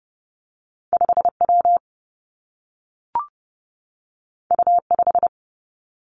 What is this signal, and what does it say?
Morse code, heard as a keyed tone, about 30 wpm, 700 Hz. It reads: 5 W U 5
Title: Two Challenging Letter-Number Combinations 30wpm